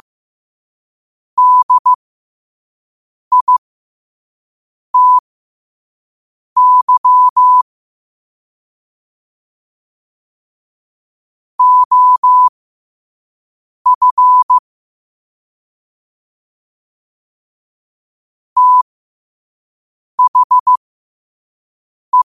Morse code, heard as a keyed tone, about 15 wpm, spaced as for 5 wpm, 1000 Hz. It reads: DITY OF THE
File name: qotdfiles / 5wpm